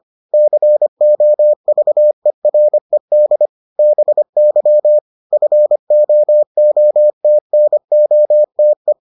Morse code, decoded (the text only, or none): COVERED BY FOOTNOTE